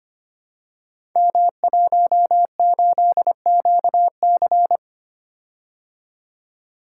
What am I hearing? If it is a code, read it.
Morse code, heard as a keyed tone, about 26 wpm, 700 Hz. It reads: M18QC
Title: Code Groups - 5 Letter:Numbers - 25wpm